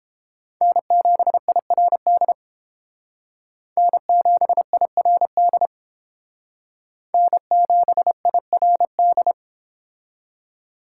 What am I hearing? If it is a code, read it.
Morse code, heard as a keyed tone, about 30 wpm, 700 Hz. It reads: N7IRD N7IRD N7IRD